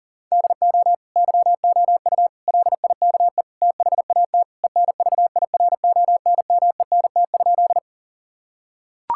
Morse code, heard as a keyed tone, about 40 wpm, 700 Hz. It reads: DO YOU LIKE THAT ENVIRONMENT?
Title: Sentences from Top 1000 Words - Review Part 5 - 40wpm